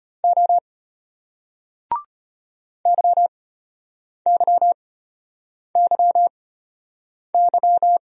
Morse code, heard as a keyed tone, about 33 wpm, spaced as for 8 wpm, 700 Hz. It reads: O YYYY